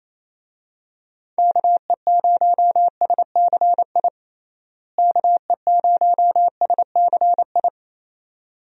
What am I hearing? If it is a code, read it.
Morse code, heard as a keyed tone, about 27 wpm, 700 Hz. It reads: KE0SCI KE0SCI
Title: USA Call Signs - 6 Characters - Repeated 2x - 28wpm